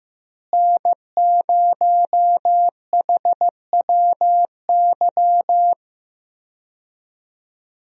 Morse code, heard as a keyed tone, about 15 wpm, 700 Hz. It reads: N0HWY
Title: USA Call Signs - 5 Characters - Repeated 3x - 15wpm